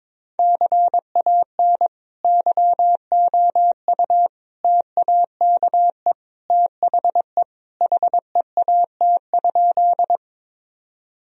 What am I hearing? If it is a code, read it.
Morse code, heard as a keyed tone, about 22 wpm, 700 Hz. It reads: CAN YOU TAKE THE HEAT?
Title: Sentences from Top 1000 Words - Review Part 3 - 22wpm